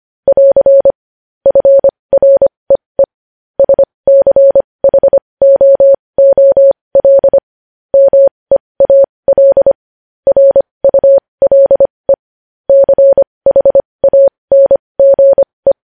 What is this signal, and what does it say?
Morse code, heard as a keyed tone, about 25 wpm, spaced as for 20 wpm, 550 Hz. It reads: <AR> FREE SCHOOL MEAL RULE CHANGE
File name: podcast / News headlines-1x-25x20